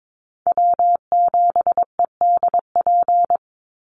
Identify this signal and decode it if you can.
Morse code, heard as a keyed tone, about 22 wpm, 700 Hz. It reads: W7EDP
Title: USA Call Signs 22wpm